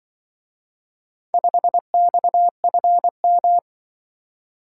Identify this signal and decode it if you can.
Morse code, heard as a keyed tone, about 24 wpm, 700 Hz. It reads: 5XFM
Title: Speed-Racing - Four Letter-Number Combinations - 20wpm